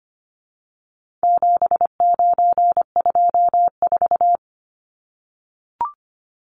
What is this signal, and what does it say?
Morse code, heard as a keyed tone, about 25 wpm, 700 Hz. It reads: 7924
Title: Numbers - 4-Digit 25wpm